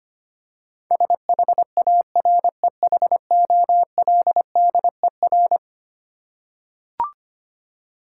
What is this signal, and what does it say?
Morse code, heard as a keyed tone, about 25 wpm, 700 Hz. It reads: SHAREHOLDER
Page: morse-code-ninja-practice-25wpm